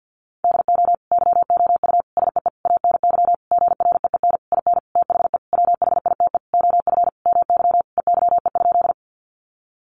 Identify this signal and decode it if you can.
Morse code, heard as a keyed tone, about 50 wpm, 700 Hz. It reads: DO YOU SEE ANY GREEN IN THE WHITE OF MY EYE?
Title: Sentences from Top 500 Words 50wpm